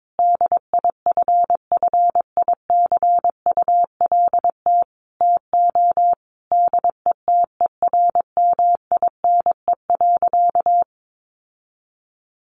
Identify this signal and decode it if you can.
Morse code, heard as a keyed tone, about 22 wpm, 700 Hz. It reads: DIFFICULT TO DETERMINE.
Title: Sentences from Top 1500 Words 22wpm